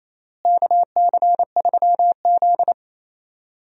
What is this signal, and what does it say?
Morse code, heard as a keyed tone, about 28 wpm, 700 Hz. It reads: KC3Z
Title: USA Call Signs - 4 Characters - Repeated 3x - 28wpm